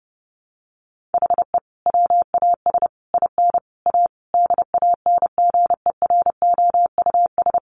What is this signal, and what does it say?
Morse code, heard as a keyed tone, about 30 wpm, 700 Hz. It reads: HE WAS IN A DANGEROUS
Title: Sentences from Top 1200 Words 30wpm